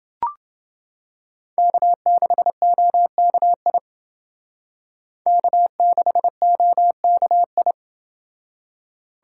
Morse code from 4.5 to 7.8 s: K6OKI

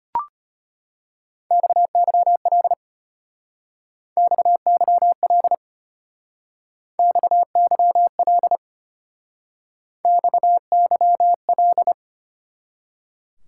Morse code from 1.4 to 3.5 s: XYL

